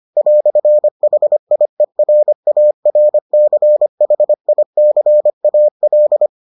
Morse code, heard as a keyed tone, about 25 wpm, 600 Hz. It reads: "HIERARCHICAL